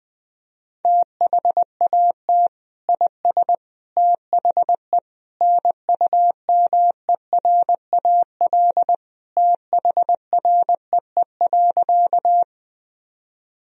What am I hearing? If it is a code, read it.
Morse code, heard as a keyed tone, about 21 wpm, 700 Hz. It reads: THAT IS THE NUMERAL THREE.
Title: Sentences from Top 1000 Words - Review Part 2 - 20wpm